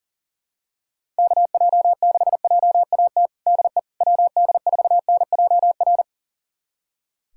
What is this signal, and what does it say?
Morse code, heard as a keyed tone, about 40 wpm, 700 Hz. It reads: KJ6JAT DE WD4NJR
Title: USA Callsign DE Callsign - 6 Characters - 40wpm